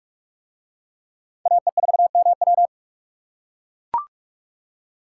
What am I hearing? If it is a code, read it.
Morse code, heard as a keyed tone, about 44 wpm, 700 Hz. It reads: AE4MW